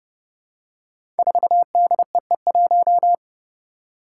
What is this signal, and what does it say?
Morse code, heard as a keyed tone, about 30 wpm, 700 Hz. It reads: 4DEE1
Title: Five Challenging Letter-Number Combinations 30wpm